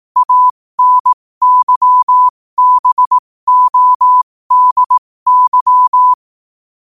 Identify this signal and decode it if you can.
Morse code, heard as a keyed tone, about 18 wpm, spaced as for 15 wpm, 1000 Hz. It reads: ANYBODY
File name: qotdfiles / 15wpm